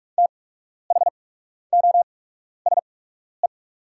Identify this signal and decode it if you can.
Morse code, heard as a keyed tone, about 44 wpm, spaced as for 12 wpm, 690 Hz. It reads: THOSE
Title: 8x Character Spacing - Top 500 Words - 45wpm